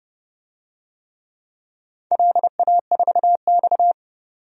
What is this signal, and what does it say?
Morse code, heard as a keyed tone, about 30 wpm, 700 Hz. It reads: LA4X
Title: DXCC-100 Call Signs - 4 Characters - 30wpm